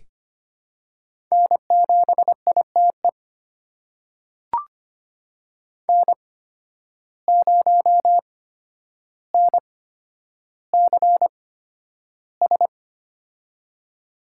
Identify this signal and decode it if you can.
Morse code, heard as a keyed tone, about 25 wpm, spaced as for 11 wpm, 700 Hz. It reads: N7ITE N0NCS
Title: USA Call Signs - 5 Characters - 8x Character Spacing - 25wpm